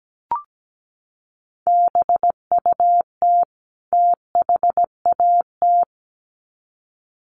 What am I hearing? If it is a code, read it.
Morse code, heard as a keyed tone, about 17 wpm, 700 Hz. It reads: BUT THAT
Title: Sets of 2 Words 17wpm